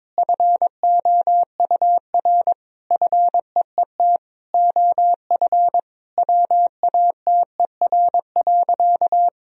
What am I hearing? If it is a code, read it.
Morse code, heard as a keyed tone, about 22 wpm, 700 Hz. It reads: FOUR FEET OF WATER.
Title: Sentences from Top 1000 Words - Review Part 1 - 22wpm